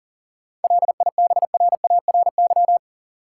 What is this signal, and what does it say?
Morse code, heard as a keyed tone, about 40 wpm, 700 Hz. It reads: LIBRARY
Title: Sentences from Top 600 Words - New Word Drill - 40wpm